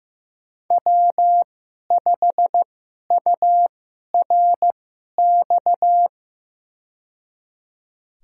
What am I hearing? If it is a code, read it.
Morse code, heard as a keyed tone, about 15 wpm, spaced as for 11 wpm, 700 Hz. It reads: W5URX
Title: USA Call Signs - 2x Character Spacing - 15wpm